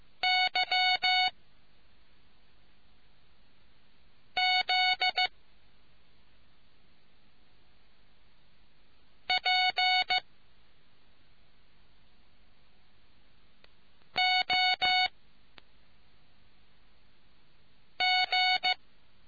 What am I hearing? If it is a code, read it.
Morse code, heard as a keyed tone, about 15 wpm, 740 Hz. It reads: Y Z P O G